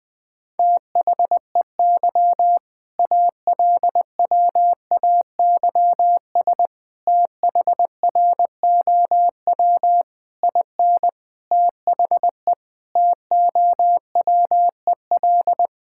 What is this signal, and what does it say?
Morse code, heard as a keyed tone, about 20 wpm, 700 Hz. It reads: THEY ALWAYS THROW IN THE TOWEL
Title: All-Age Appropriate Jokes - 20wpm